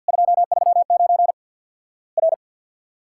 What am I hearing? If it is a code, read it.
Morse code, heard as a keyed tone, about 50 wpm, 700 Hz. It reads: 239 R
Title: Worked All Europe DX Contest - QTC Exchange - 50wpm